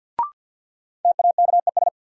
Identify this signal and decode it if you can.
Morse code, heard as a keyed tone, about 52 wpm, 700 Hz. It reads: TAKES
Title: Sentences from Top 1000 Words - Review Part 6 - New Word Drill - 50wpm